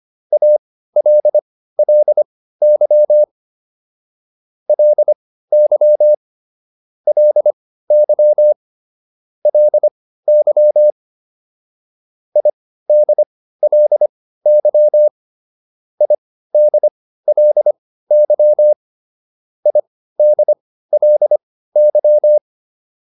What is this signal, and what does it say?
Morse code, heard as a keyed tone, about 25 wpm, spaced as for 15 wpm, 600 Hz. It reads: ALLY LY LY LY IDLY IDLY IDLY